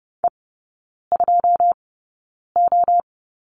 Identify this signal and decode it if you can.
Morse code, heard as a keyed tone, about 30 wpm, spaced as for 9 wpm, 700 Hz. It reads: E2O
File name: course / 30:10wpm / 087 Lesson 15e - Number 5 - Three Characters 30wpm